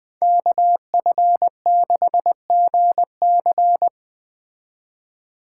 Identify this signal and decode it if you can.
Morse code, heard as a keyed tone, about 20 wpm, 700 Hz. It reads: KF6GC
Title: Speed Racing - USA 2x2 Call Signs 17wpm